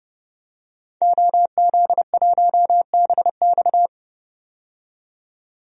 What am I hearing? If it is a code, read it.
Morse code, heard as a keyed tone, about 31 wpm, 700 Hz. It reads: OZ1BX
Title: DXCC-100 Call Signs - 5 Characters - 30wpm